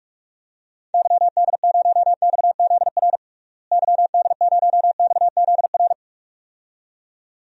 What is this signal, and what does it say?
Morse code, heard as a keyed tone, about 46 wpm, 700 Hz. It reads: YD0XZR YD0XZR
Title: DXCC-100 Call Signs - 6 Characters - Repeated 2x - 45wpm